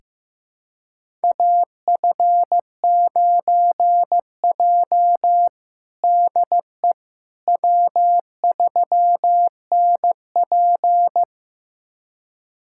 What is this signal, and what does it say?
Morse code, heard as a keyed tone, about 15 wpm, 700 Hz. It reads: AF9J DE W3NP
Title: USA Callsign DE Callsign - 4 Characters - 15wpm